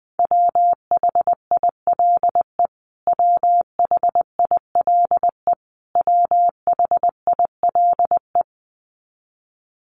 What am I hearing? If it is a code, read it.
Morse code, heard as a keyed tone, about 20 wpm, 700 Hz. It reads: WHILE WHILE WHILE
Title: Top 300 Words Repeated 20wpm